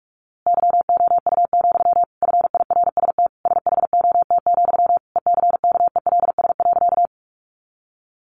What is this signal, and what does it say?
Morse code, heard as a keyed tone, about 44 wpm, 700 Hz. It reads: YOU, FIRST SHOT, ECKELS.
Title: A Sound Of Thunder 45wpm